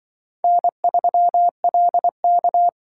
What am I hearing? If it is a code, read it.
Morse code, heard as a keyed tone, about 23 wpm, 700 Hz. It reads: N3LK
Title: Speed Racing - USA 1x2 Call Signs 20wpm